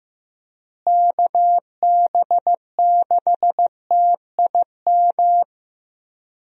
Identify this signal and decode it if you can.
Morse code, heard as a keyed tone, about 15 wpm, 700 Hz. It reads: KB6TIM
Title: USA Call Signs - 6 Characters - Repeated 3x - 15wpm